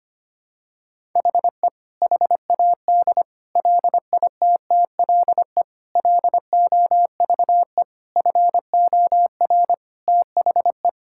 Morse code, heard as a keyed tone, about 25 wpm, 700 Hz. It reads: HE HAD LITTLE LOVE FOR THE
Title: Sentences from Top 1600 Words 25wpm